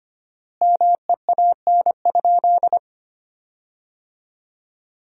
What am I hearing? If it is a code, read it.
Morse code, heard as a keyed tone, about 25 wpm, 700 Hz. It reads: MEAN?
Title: 4x Word-Spacing - Sentences from Top 200 Words - 25wpm